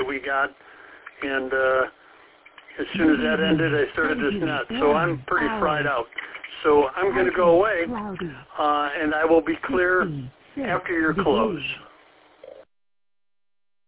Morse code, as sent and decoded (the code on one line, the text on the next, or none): none